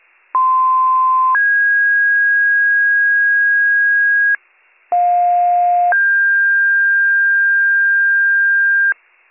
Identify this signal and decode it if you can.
Two-tone page, heard as a one-second tone, then a three-second tone, about 1040 then 1650 Hz, about 710 then 1600 Hz. It is heard from 0.3 to 8.9 s.